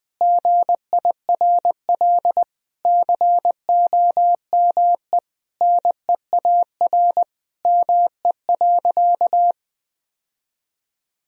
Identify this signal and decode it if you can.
Morse code, heard as a keyed tone, about 21 wpm, 700 Hz. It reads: GIRL COME NEAR ME.
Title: Sentences from Top 1000 Words - Review Part 1 - 20wpm